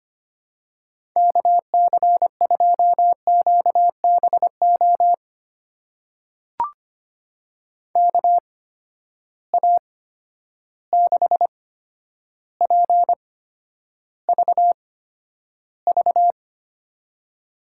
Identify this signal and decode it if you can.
Morse code, heard as a keyed tone, about 25 wpm, spaced as for 11 wpm, 700 Hz. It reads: KC2QBO KA6PVV
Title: USA Call Signs - 6 Characters - 8x Character Spacing - 25wpm